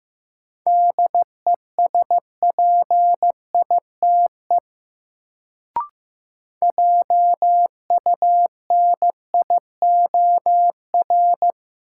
Morse code, heard as a keyed tone, about 15 wpm, 700 Hz. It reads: DESPITE JUNIOR
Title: Sentences from Top 1000 Words - Review Part 7 - New Word Drill - 15wpm